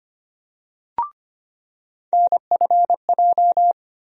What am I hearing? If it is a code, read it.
Morse code, heard as a keyed tone, about 25 wpm, 700 Hz. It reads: NFJ
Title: USA Call Sign Postfixes - 3 Letters 25wpm